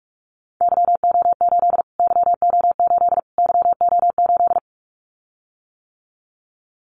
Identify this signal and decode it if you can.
Morse code, heard as a keyed tone, about 44 wpm, 700 Hz. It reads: YO8 YO8 YO8
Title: Callsign Prefixes for Top 25 DXCC - Repeated 3x - 45wpm